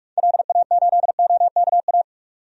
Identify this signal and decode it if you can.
Morse code, heard as a keyed tone, about 44 wpm, 700 Hz. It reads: LA8OKA
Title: DXCC-100 Call Signs - 6 Characters - 45wpm